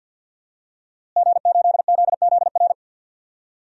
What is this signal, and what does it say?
Morse code, heard as a keyed tone, about 53 wpm, 700 Hz. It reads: G8ZZR
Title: DXCC-100 Call Signs - 5 Characters - Repeated 3x - 50wpm